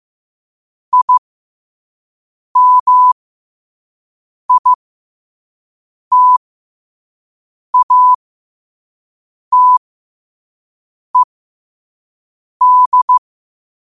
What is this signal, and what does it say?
Morse code, heard as a keyed tone, about 15 wpm, spaced as for 5 wpm, 1000 Hz. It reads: IMITATED
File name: qotdfiles / 5wpm